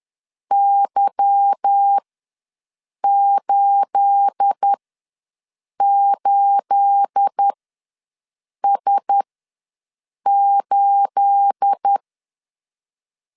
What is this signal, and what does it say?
Morse code, heard as a keyed tone, about 10 wpm, spaced as for 6 wpm, 800 Hz. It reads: Y88S8
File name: LEZIONI / 9 28 ws 10wpm 4124